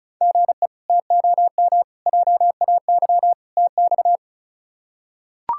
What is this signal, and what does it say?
Morse code, heard as a keyed tone, about 34 wpm, 700 Hz. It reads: GE TOM JAY TX
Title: K1USN SST QSO Practice - 35wpm